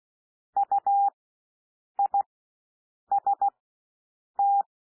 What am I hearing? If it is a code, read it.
Morse code, heard as a keyed tone, about 16 wpm, spaced as for 7 wpm, 800 Hz. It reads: UIST
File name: cwmp3 / W1 SOUND WORDS 16 wpm